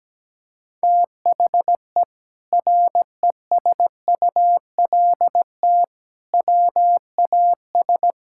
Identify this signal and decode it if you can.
Morse code, heard as a keyed tone, about 17 wpm, 700 Hz. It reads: THE RESULT WAS
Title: Sentences from Top 1000 Words - Review Part 6 - 17wpm